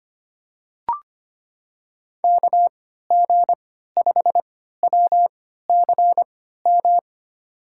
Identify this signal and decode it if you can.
Morse code, heard as a keyed tone, about 25 wpm, spaced as for 14 wpm, 700 Hz. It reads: KG5WCM
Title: USA Call Signs - 3x Character Spacing - 25wpm